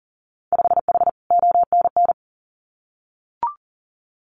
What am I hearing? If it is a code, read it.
Morse code, heard as a keyed tone, about 42 wpm, 700 Hz. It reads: 5H ONN
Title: Winter Field Day Exchanges Encore 40wpm